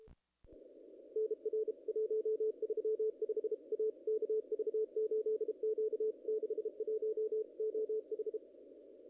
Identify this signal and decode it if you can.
Morse code, heard as a keyed tone, about 32 wpm, 440 Hz. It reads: NR 135 A KV8Q 61 OH
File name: contests / ss-cw-k5tr-op-n5rz -0000